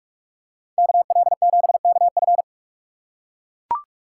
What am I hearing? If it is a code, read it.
Morse code, heard as a keyed tone, about 43 wpm, 700 Hz. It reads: KL7KF